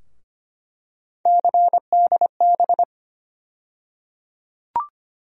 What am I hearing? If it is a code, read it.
Morse code, heard as a keyed tone, about 25 wpm, 700 Hz. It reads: CDB
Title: USA Call Sign Postfixes - 3 Letters 25wpm